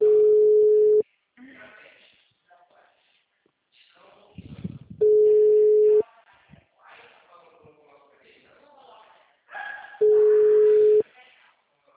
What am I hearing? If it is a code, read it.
Ringback tone (Europe, 425 Hz), with speech.